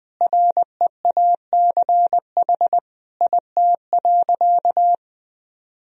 Morse code, heard as a keyed tone, about 20 wpm, 700 Hz. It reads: REACH IT.
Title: Sentences from Top 400 Words and Abbreviations 20wpm